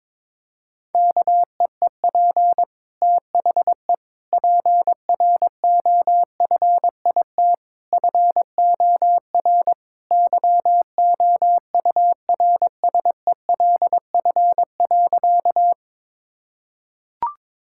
Morse code, heard as a keyed tone, about 22 wpm, 700 Hz. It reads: KEEP THE PROFIT FOR YOURSELF.
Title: Sentences from Top 1000 Words - Review Part 5 - 22wpm